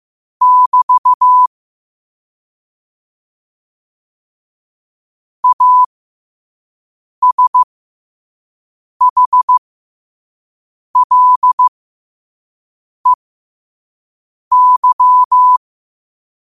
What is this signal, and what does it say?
Morse code, heard as a keyed tone, about 15 wpm, spaced as for 5 wpm, 1000 Hz. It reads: <BT> ASHLEY
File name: qotdfiles / 5wpm